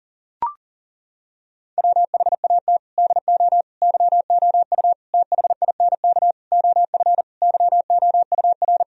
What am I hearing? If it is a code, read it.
Morse code, heard as a keyed tone, about 39 wpm, 700 Hz. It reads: WHAT DO YOU THINK OF YOUR
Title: Sentences from Top 1000 Words - Review Part 4 - 40wpm